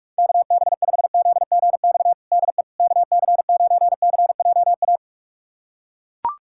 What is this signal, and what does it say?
Morse code, heard as a keyed tone, about 43 wpm, 700 Hz. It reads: KB5ZGX DE KC9CJA
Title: USA Callsign DE Callsign - 6 Characters - 45wpm